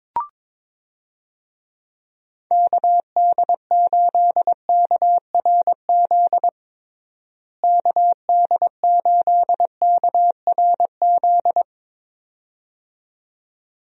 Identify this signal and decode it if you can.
Morse code, heard as a keyed tone, about 22 wpm, 700 Hz. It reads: KD8KRZ KD8KRZ
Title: USA Call Signs - 6 Characters - Repeated 2x - 22wpm